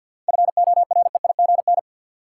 Morse code, heard as a keyed tone, about 52 wpm, 700 Hz. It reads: FOREIGN
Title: Sentences from Top 1400 Words - New Word Drill Alt - 50wpm